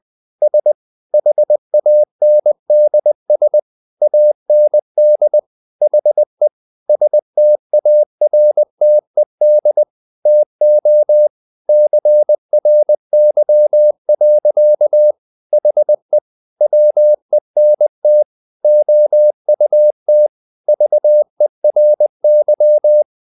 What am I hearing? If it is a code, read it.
Morse code, heard as a keyed tone, about 20 wpm, 600 Hz. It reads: S HANDS AND HE STARTED TO CRY. HE WENT OUT VERY
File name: the old man and the sea part 2 20wpm